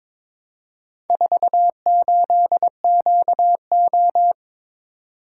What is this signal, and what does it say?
Morse code, heard as a keyed tone, about 22 wpm, 700 Hz. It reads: 48QO
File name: practice / 22wpm / Four Letter-Number Combinations - 22wpm